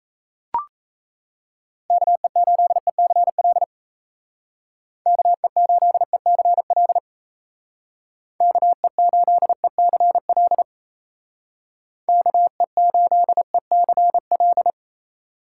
Morse code, heard as a keyed tone, about 37 wpm, 700 Hz. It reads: KE8ECL KE8ECL KE8ECL KE8ECL